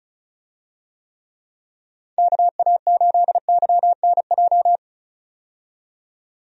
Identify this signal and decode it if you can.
Morse code, heard as a keyed tone, about 34 wpm, 700 Hz. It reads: KA8YNJ
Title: USA Call Signs - 6 Characters - 35wpm